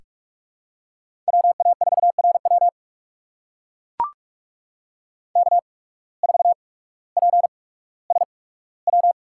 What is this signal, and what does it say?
Morse code, heard as a keyed tone, about 44 wpm, spaced as for 19 wpm, 700 Hz. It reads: WA4RW K4PSW
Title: USA Call Signs - 5 Characters - 8x Character Spacing - 45wpm